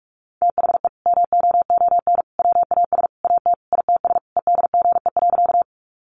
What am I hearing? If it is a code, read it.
Morse code, heard as a keyed tone, about 44 wpm, 700 Hz. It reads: THE MOON WAS AT ITS EDGE.
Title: Sentences from Top 1300 Words 45wpm